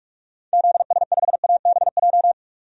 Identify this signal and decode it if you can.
Morse code, heard as a keyed tone, about 41 wpm, 700 Hz. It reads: ZS5ABJ
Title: DXCC-100 Call Signs - 6 Characters - 45wpm